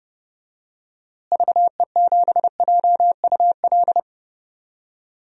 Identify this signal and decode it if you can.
Morse code, heard as a keyed tone, about 30 wpm, 700 Hz. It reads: VE7JUL
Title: DXCC-100 Call Signs - 6 Characters - Repeated 3x - 30wpm